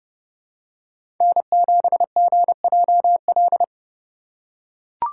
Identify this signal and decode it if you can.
Morse code, heard as a keyed tone, about 31 wpm, 700 Hz. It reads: N7GJL
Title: USA Call Signs - 3x Character Spacing - 30wpm